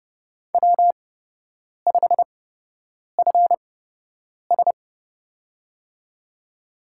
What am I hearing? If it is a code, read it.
Morse code, heard as a keyed tone, about 30 wpm, spaced as for 8 wpm, 700 Hz. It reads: W5FS
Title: USA Call Signs - 8x Character Spacing - 30wpm